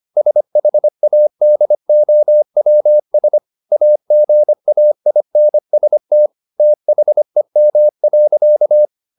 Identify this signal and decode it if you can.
Morse code, heard as a keyed tone, about 25 wpm, 600 Hz. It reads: SHADOWS AGAINST THEM.